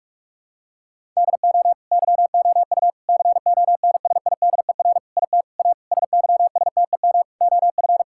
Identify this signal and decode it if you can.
Morse code, heard as a keyed tone, about 41 wpm, 700 Hz. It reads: DO YOU CONSIDER IT A SYSTEM OF
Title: Sentences from Top 1000 Words - Review Part 6 - 45wpm